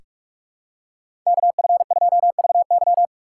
Morse code, heard as a keyed tone, about 42 wpm, 700 Hz. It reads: KF2VY
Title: USA Call Signs - 5 Characters - 2x Character Spacing - 45wpm